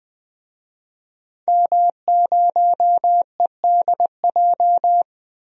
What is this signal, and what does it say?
Morse code, heard as a keyed tone, about 20 wpm, 700 Hz. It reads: M0EDJ
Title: DXCC-100 Call Signs - 5 Characters - Repeated 3x - 20wpm